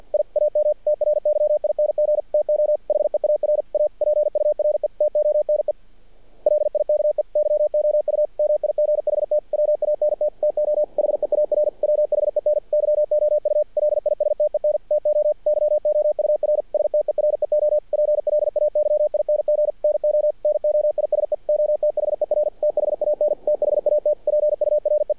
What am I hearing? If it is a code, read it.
Morse code, heard as a keyed tone, about 50 wpm, 600 Hz. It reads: I AM TRYING TO HEAR A PURE TONE LIKE YOU MIGHT WANT TO HEAR WHEN YOU LISTEN TO YOUR STEREO PLAYING NO NOISE OTHER THAN THAT PURE
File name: archived / CRSnet